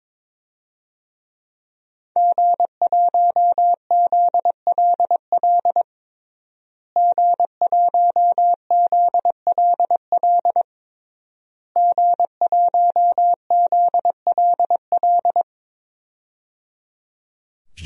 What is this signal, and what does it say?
Morse code, heard as a keyed tone, about 22 wpm, 700 Hz. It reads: G1ZLL G1ZLL G1ZLL
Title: DXCC-100 Call Signs - 5 Characters - Repeated 3x - 22wpm